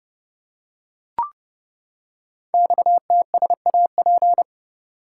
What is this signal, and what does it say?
Morse code, heard as a keyed tone, about 30 wpm, 690 Hz. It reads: XTSAP
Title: Code Groups - 5 Letters - 30wpm